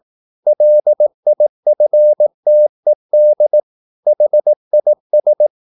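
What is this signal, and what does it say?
Morse code, heard as a keyed tone, about 18 wpm, 600 Hz. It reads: LIFTED HIS
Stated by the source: Morse practice at 18 wpm, English text, from CW med Gnister.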